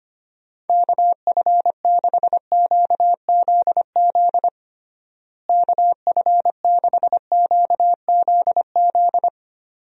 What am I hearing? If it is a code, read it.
Morse code, heard as a keyed tone, about 25 wpm, 700 Hz. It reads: KF6QZZ KF6QZZ
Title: USA Call Signs - 6 Characters - Repeated 2x - 25wpm